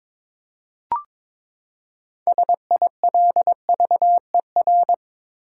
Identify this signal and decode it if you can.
Morse code, heard as a keyed tone, about 23 wpm, 700 Hz. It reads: SILVER
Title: Six Letter Words - 22wpm